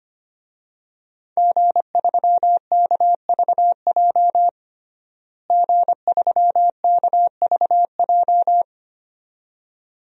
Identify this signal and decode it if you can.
Morse code, heard as a keyed tone, about 25 wpm, 700 Hz. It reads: G3KVJ G3KVJ
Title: DXCC-100 Call Signs - 5 Characters - Repeated 2x - 25wpm